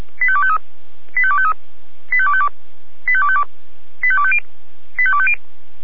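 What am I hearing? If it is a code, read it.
Five-tone selective call (ZVEI), heard as quick runs of five tones, tones of about 70 ms. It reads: 85253 85253 85252 85252 85250 85250
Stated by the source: Five-tone ZVEI selective call received by SDR.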